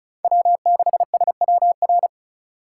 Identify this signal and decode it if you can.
Morse code, heard as a keyed tone, about 34 wpm, 700 Hz. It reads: W6SWR
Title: USA Call Signs - 5 Characters - 3x Character Spacing - 35wpm